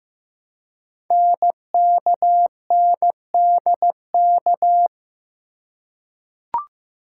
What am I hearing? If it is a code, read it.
Morse code, heard as a keyed tone, about 15 wpm, 700 Hz. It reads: NKNDK
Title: Code Groups - 5 Letters - 15wpm